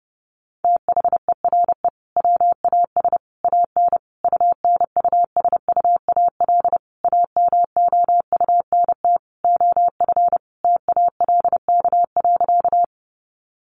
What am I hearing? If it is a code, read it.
Morse code, heard as a keyed tone, about 30 wpm, 700 Hz. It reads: THERE WAS AN UNUSUAL AMOUNT OF TALK.
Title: Sentences from Top 1400 Words 30wpm